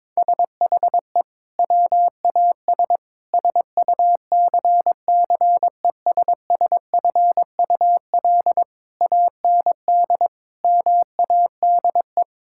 Morse code, heard as a keyed tone, about 22 wpm, 700 Hz. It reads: SHE WAS SUCCESSFUL AND MADE
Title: Sentences from Top 1000 Words - Review Part 6 - 22wpm